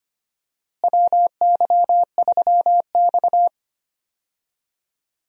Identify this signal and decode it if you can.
Morse code, heard as a keyed tone, about 25 wpm, 700 Hz. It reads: WY3X
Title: USA Call Signs - 4 Characters - Repeated 2x - 25wpm